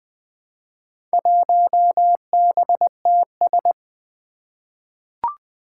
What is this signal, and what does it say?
Morse code, heard as a keyed tone, about 20 wpm, 700 Hz. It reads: 1BTS
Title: Four Challenging Letter-Number Combinations 20wpm